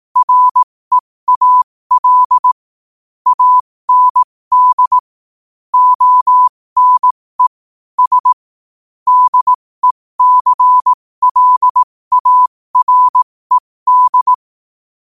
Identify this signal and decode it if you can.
Morse code, heard as a keyed tone, about 18 wpm, spaced as for 15 wpm, 1000 Hz. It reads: REAL AND ONE S DECLARED
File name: qotdfiles / 15wpm